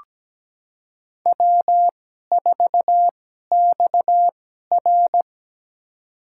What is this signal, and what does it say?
Morse code, heard as a keyed tone, about 17 wpm, spaced as for 12 wpm, 700 Hz. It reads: W4XR
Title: USA Call Signs - 2x Character Spacing - 17wpm